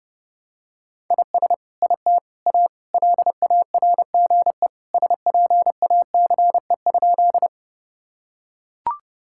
Morse code, heard as a keyed tone, about 30 wpm, 700 Hz. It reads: IS IT A LARGE SPACE?
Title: Sentences from Top 1000 Words - Review Part 5 - 30wpm